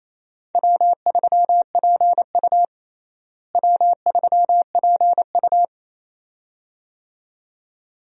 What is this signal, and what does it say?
Morse code, heard as a keyed tone, about 28 wpm, 700 Hz. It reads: W3PU W3PU
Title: USA Call Signs - 4 Characters - Repeated 2x - 28wpm